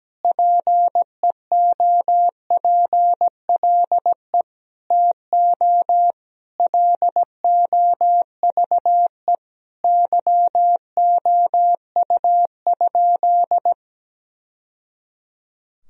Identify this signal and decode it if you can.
Morse code, heard as a keyed tone, about 17 wpm, 700 Hz. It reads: PEOPLE TO LOVE YOU?
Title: Sentences from Top 1000 Words - Review Part 3 - 17wpm